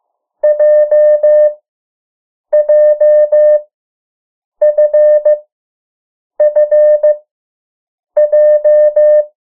Morse code, heard as a keyed tone, about 15 wpm, 600 Hz. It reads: J J F F J
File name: morse mp3 / 03 urok 2